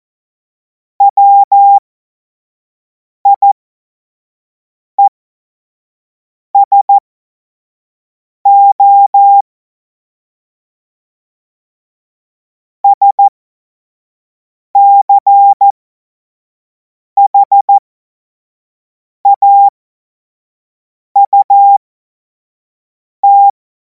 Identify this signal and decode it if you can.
Morse code, heard as a keyed tone, about 14 wpm, spaced as for 5 wpm, 800 Hz. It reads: WIESO SCHAUT
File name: Morsekurs / Witze / Witz 05. Mai 2026 25 F70